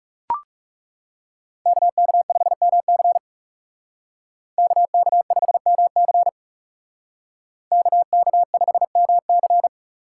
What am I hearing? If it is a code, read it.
Morse code, heard as a keyed tone, about 35 wpm, 700 Hz. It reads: KK5MC KK5MC KK5MC
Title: Speed Racing - USA 2x2 Call Signs 30wpm